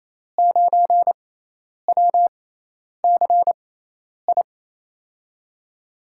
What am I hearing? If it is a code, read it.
Morse code, heard as a keyed tone, about 28 wpm, spaced as for 10 wpm, 700 Hz. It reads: 9WCI
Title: USA Call Signs - 5 Characters - 6x Character Spacing - 28wpm